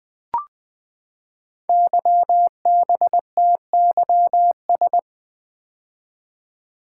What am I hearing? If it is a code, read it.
Morse code, heard as a keyed tone, about 20 wpm, 700 Hz. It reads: YBTYS
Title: Code Groups - 5 Letters - 20wpm